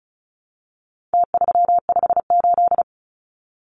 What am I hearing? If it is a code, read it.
Morse code, heard as a keyed tone, about 34 wpm, 700 Hz. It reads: T358
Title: Numbers - Contest Style - 4-Digit 35wpm